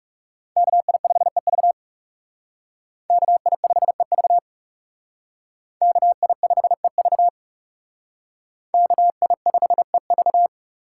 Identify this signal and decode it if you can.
Morse code, heard as a keyed tone, about 36 wpm, 700 Hz. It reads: KI5EV KI5EV KI5EV KI5EV
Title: Speed Racing - USA 2x2 Call Signs 30wpm